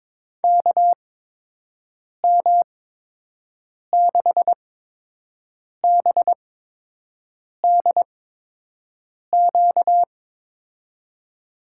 Morse code, heard as a keyed tone, about 22 wpm, spaced as for 6 wpm, 700 Hz. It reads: KM6BDQ
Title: USA Call Signs - 6 Characters - 8x Character Spacing - 22wpm